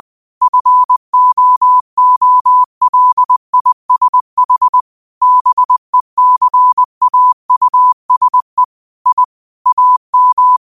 Morse code, heard as a keyed tone, about 20 wpm, 1000 Hz. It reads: FOOLISH BECAUSE I AM
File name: qotdfiles / 20wpm